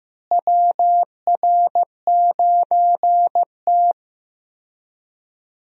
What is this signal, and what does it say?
Morse code, heard as a keyed tone, about 15 wpm, 700 Hz. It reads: WR9T
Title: USA Call Signs - 4 Characters - Repeated 2x - 15wpm